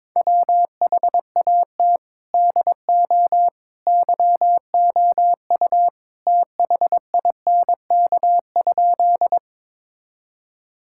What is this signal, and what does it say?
Morse code, heard as a keyed tone, about 21 wpm, 700 Hz. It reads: WHAT DO YOU THINK?
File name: practice / 22wpm / Sentences with 2-Word Phrases - 22wpm